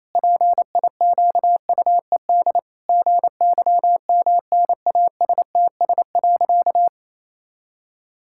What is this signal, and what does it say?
Morse code, heard as a keyed tone, about 28 wpm, 700 Hz. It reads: PIQUED GYMNASTS.
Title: Pangram Challenge v2 - 28wpm